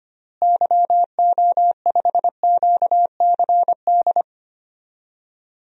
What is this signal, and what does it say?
Morse code, heard as a keyed tone, about 25 wpm, 700 Hz. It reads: YO5QCD
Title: DXCC-100 Call Signs - 6 Characters - Repeated 2x - 25wpm